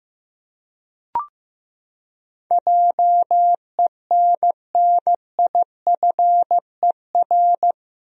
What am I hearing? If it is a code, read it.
Morse code, heard as a keyed tone, about 15 wpm, 700 Hz. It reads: JENNIFER
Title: USA Common Names - 15wpm